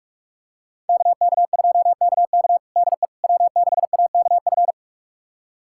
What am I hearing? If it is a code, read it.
Morse code, heard as a keyed tone, about 42 wpm, 700 Hz. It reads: KK2KK DE W6AKF
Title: USA Callsign DE Callsign - 5 Characters - 45wpm